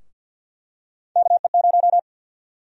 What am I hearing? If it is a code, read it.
Morse code, heard as a keyed tone, about 52 wpm, 700 Hz. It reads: KE0
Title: USA Call Sign Prefixes - 50wpm